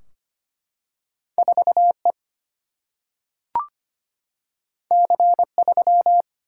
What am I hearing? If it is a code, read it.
Morse code, heard as a keyed tone, about 25 wpm, 700 Hz. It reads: 4E C3
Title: Single-Letter Single-Number 25wpm